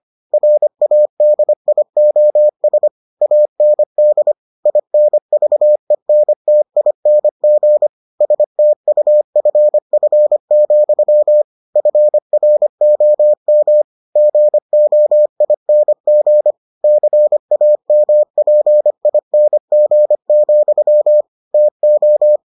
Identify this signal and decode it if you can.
Morse code, heard as a keyed tone, about 25 wpm, 600 Hz. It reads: RADIOS AND INVENTING STUFF, FROM GOING CAMPING, TO